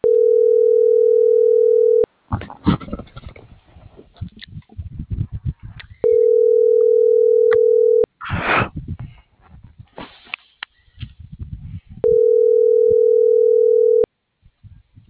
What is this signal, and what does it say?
Ringback tone (North America), 440 + 480 Hz.